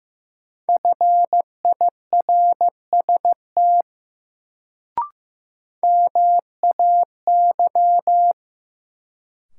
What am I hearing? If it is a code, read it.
Morse code, heard as a keyed tone, about 15 wpm, 700 Hz. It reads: FIRST MAY
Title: Sentences from Top 100 Words - New Word Drill - 15wpm